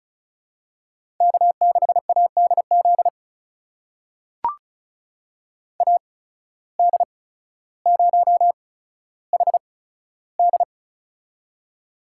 Morse code, heard as a keyed tone, about 36 wpm, spaced as for 15 wpm, 700 Hz. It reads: K6ADZ AD0HD
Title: USA Call Signs - 5 Characters - 8x Character Spacing - 35wpm